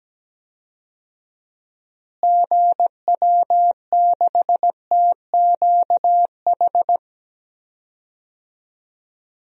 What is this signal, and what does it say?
Morse code, heard as a keyed tone, about 17 wpm, 700 Hz. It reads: GW6TQH
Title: DXCC-100 Call Signs - 6 Characters - Repeated 2x - 17wpm